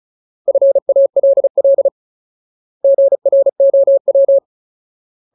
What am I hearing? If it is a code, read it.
Morse code, heard as a keyed tone, about 34 wpm, 550 Hz. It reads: FALL GROW
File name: Ultimate Word List / 35WPM Ultimate